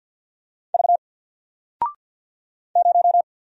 Morse code, heard as a keyed tone, about 50 wpm, 700 Hz. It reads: V 0
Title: Single Character (Letter:Number:Punctuation:Prosign) 50wpm